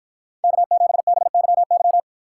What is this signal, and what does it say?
Morse code, heard as a keyed tone, about 50 wpm, 700 Hz. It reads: K7BYY